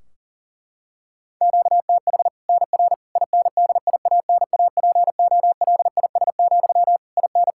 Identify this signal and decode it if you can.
Morse code, heard as a keyed tone, about 40 wpm, 700 Hz. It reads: QTH NR INDIANAPOLIS, IN
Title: Practice QSO - Exchange 1 - Beginning - 40wpm